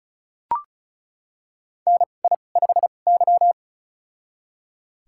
Morse code, heard as a keyed tone, about 36 wpm, spaced as for 25 wpm, 700 Hz. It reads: NI5Y
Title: USA Call Signs - 2x Character Spacing - 35wpm